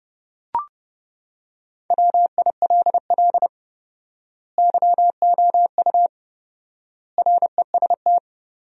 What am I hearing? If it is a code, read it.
Morse code, heard as a keyed tone, about 30 wpm, 700 Hz. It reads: WILL YOU REST
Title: 4x Word-Spacing - Sentences from Top 500 Words - 30wpm